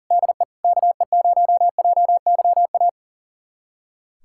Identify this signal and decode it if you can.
Morse code, heard as a keyed tone, about 40 wpm, 700 Hz. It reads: DE KE0JYA
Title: USA Callsign DE Callsign 40wpm